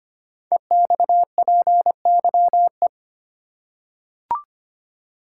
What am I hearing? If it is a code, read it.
Morse code, heard as a keyed tone, about 25 wpm, 700 Hz. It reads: EXPYE
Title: Code Groups - 5 Letters - 25wpm